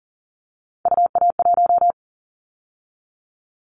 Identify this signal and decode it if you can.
Morse code, heard as a keyed tone, about 40 wpm, 700 Hz. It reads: UA1
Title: Callsign Prefixes for Top 25 DXCC - 40wpm